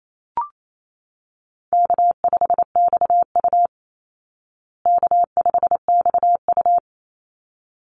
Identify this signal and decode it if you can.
Morse code, heard as a keyed tone, about 28 wpm, 700 Hz. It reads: K5XU K5XU
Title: Easy Call Signs Repeated 2x - 28wpm